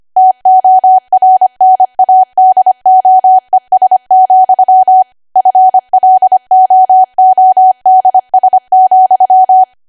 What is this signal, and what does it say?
Morse code, heard as a keyed tone, about 25 wpm, 750 Hz. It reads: TORNADOES, FLOODS,
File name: Archive / 25 WPM / 25WPM